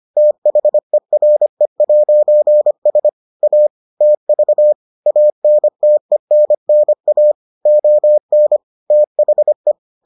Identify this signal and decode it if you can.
Morse code, heard as a keyed tone, about 25 wpm, 600 Hz. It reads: THERE'S A TV ANTENNA ON THE